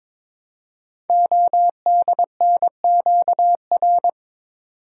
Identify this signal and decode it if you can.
Morse code, heard as a keyed tone, about 23 wpm, 700 Hz. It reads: ODNQR
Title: Code Groups - 5 Letters - 22wpm